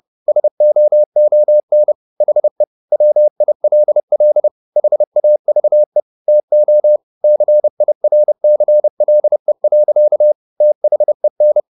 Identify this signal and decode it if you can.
Morse code, heard as a keyed tone, about 30 wpm, 600 Hz. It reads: SOON HE WILL HAVE TO CIRCLE. THEN